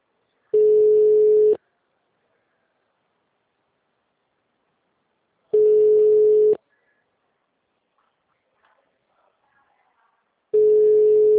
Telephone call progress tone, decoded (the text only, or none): ringback tone